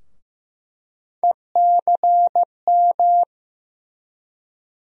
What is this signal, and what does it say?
Morse code, heard as a keyed tone, about 15 wpm, 700 Hz. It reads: ECM
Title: USA Call Sign Postfixes - 3 Letters 15wpm